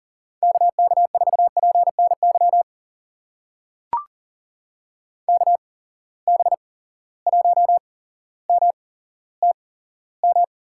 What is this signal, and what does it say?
Morse code, heard as a keyed tone, about 39 wpm, spaced as for 17 wpm, 700 Hz. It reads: KK4PNY KB1MTM
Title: USA Call Signs - 6 Characters - 8x Character Spacing - 40wpm